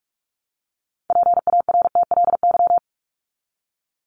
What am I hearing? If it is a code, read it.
Morse code, heard as a keyed tone, about 44 wpm, 700 Hz. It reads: PARTLY
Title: LY Suffix - Top 100 Words with Base Words 45wpm